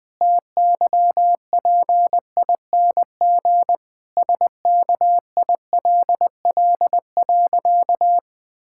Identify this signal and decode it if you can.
Morse code, heard as a keyed tone, about 20 wpm, 700 Hz. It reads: TYPING SKILL.
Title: Pangram Challenge v2 - 20wpm